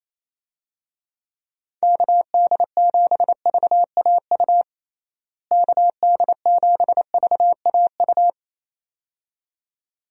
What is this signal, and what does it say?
Morse code, heard as a keyed tone, about 28 wpm, 700 Hz. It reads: KD7VAU KD7VAU
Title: USA Call Signs - 6 Characters - Repeated 2x - 28wpm